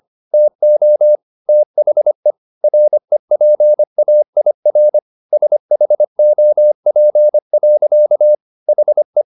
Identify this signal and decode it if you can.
Morse code, heard as a keyed tone, about 25 wpm, 600 Hz. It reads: TO THE REPAIR SHOP. HE